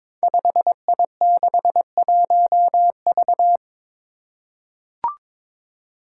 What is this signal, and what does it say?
Morse code, heard as a keyed tone, about 22 wpm, 700 Hz. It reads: 5I61V
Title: Five Challenging Letter-Number Combinations 22wpm